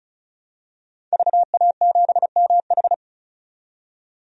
Morse code, heard as a keyed tone, about 36 wpm, 700 Hz. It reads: VA7MH
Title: DXCC-100 Call Signs - 5 Characters - 35wpm